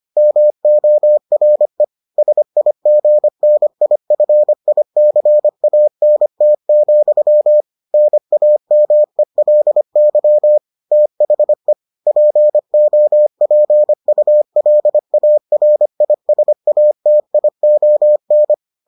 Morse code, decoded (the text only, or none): MORE SIGNIFICANT, NAMELY THE POPULARISATION